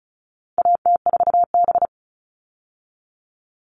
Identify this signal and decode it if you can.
Morse code, heard as a keyed tone, about 34 wpm, 700 Hz. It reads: AT4B
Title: All Cut Numbers - 4-Digit 35wpm